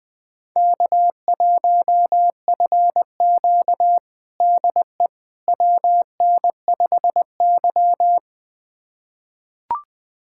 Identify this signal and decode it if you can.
Morse code, heard as a keyed tone, about 20 wpm, 700 Hz. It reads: K1FQ DE WN5Y
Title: USA Callsign DE Callsign - 4 Characters - 20wpm